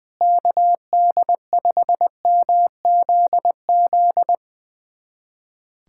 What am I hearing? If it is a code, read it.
Morse code, heard as a keyed tone, about 21 wpm, 700 Hz. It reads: KD5MZZ
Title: USA Callsign DE Callsign - 6 Characters - 20wpm